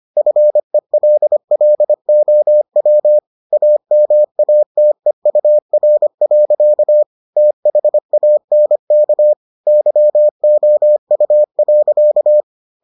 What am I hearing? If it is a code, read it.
Morse code, heard as a keyed tone, about 25 wpm, 600 Hz. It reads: FELLOW AMATEUR. THANK YOU.